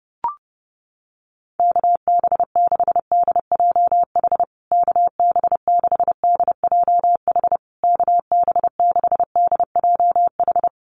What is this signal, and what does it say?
Morse code, heard as a keyed tone, about 30 wpm, 700 Hz. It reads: KB6DJH KB6DJH KB6DJH
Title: USA Call Signs Repeated 3x 30wpm